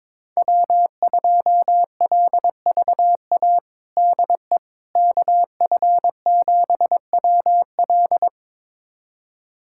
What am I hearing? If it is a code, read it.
Morse code, heard as a keyed tone, about 23 wpm, 700 Hz. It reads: W2LVA DE KF7WL
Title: USA Callsign DE Callsign - 5 Characters - 22wpm